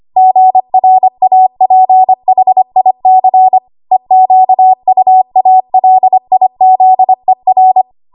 Morse code, heard as a keyed tone, about 25 wpm, 750 Hz. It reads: GRAPHIC EQUALIZER
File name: Morse / 25WPM